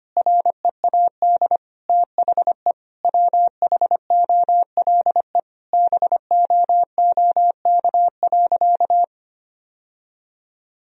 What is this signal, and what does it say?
Morse code, heard as a keyed tone, about 25 wpm, 700 Hz. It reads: READ THE WHOLE BOOK.
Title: Sentences from Top 1000 Words - Review Part 1 - 25wpm